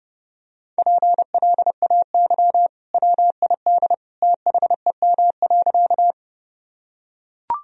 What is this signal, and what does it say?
Morse code, heard as a keyed tone, about 30 wpm, 700 Hz. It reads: PLAY WID THEM.
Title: Sentences from Top 200 Words and Abbreviations 30wpm